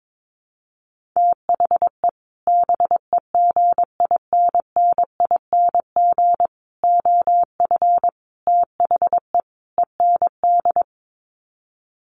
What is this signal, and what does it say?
Morse code, heard as a keyed tone, about 22 wpm, 700 Hz. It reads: THE BEGINNING OF THE END
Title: Sets of 5 Words Encore2 22wpm